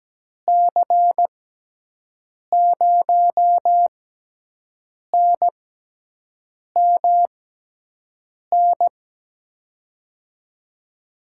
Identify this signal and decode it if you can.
Morse code, heard as a keyed tone, about 17 wpm, spaced as for 6 wpm, 700 Hz. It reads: C0NMN
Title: USA Call Signs - 6 Characters - 6x Character Spacing - 17wpm